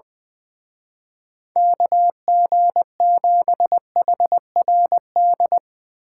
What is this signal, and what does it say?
Morse code, heard as a keyed tone, about 20 wpm, 700 Hz. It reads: KG7HRD
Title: Speed Racing - USA Call Signs 20wpm